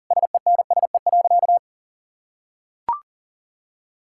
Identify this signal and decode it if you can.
Morse code, heard as a keyed tone, about 40 wpm, 700 Hz. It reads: SENSE.
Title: Sentences from Top 1000 Words - Review Part 5 - 40wpm